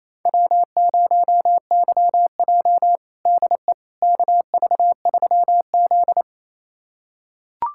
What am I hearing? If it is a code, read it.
Morse code, heard as a keyed tone, about 28 wpm, 700 Hz. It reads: W0YJ DE KV3Z
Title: USA Callsign DE Callsign - 4 Characters - 28wpm